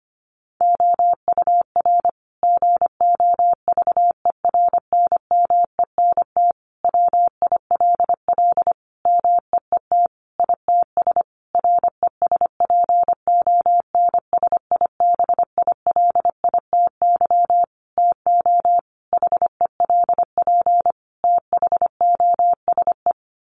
Morse code, heard as a keyed tone, about 25 wpm, 700 Hz. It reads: OUR GOVERNMENT WILL MEET ITS RESPONSIBILITY TO HELP THOSE
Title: Sentences from Top 1200 Words 25wpm